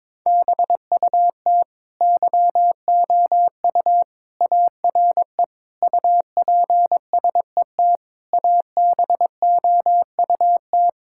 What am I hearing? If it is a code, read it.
Morse code, heard as a keyed tone, about 22 wpm, 700 Hz. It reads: BUT YOU ARE UPSET ABOUT
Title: Sentences from Top 1000 Words - Review Part 5 - 22wpm